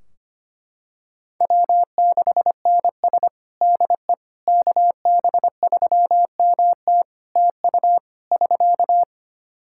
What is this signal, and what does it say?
Morse code, heard as a keyed tone, about 25 wpm, 700 Hz. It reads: W6NS DE KB3MT TU <SK>
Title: Practice QSO - Exchange 4 - Ending - 25wpm